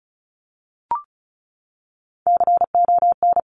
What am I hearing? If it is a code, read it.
Morse code, heard as a keyed tone, about 36 wpm, 700 Hz. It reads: CON
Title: Top Three Letter Combinations in Words 35wpm